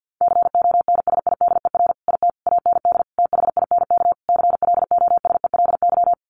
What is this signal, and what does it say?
Morse code, heard as a keyed tone, about 50 wpm, 700 Hz. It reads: CONSIDER IT AND THINK CLOSELY
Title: Sentences from Top 1400 Words 50wpm